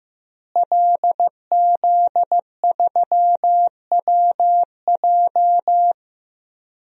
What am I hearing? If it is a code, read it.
Morse code, heard as a keyed tone, about 15 wpm, 700 Hz. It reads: LZ3WJ
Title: DXCC-100 Call Signs - 5 Characters - Repeated 2x - 15wpm